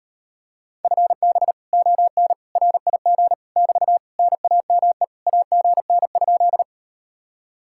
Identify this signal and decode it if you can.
Morse code, heard as a keyed tone, about 38 wpm, 700 Hz. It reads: FB ON RIG <BT> NAME AGN?